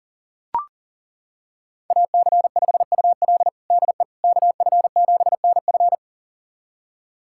Morse code, heard as a keyed tone, about 40 wpm, 700 Hz. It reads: AC5UL DE KF7NF
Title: USA Callsign DE Callsign 40wpm